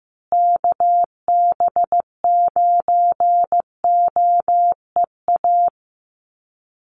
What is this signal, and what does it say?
Morse code, heard as a keyed tone, about 15 wpm, 700 Hz. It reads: KB9OEA
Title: USA Call Signs Repeated 2x 15wpm